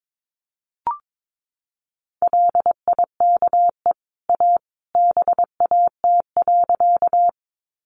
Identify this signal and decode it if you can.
Morse code, heard as a keyed tone, about 22 wpm, 700 Hz. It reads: LIKE A BAT.
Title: Sentences from Top 1500 Words 22wpm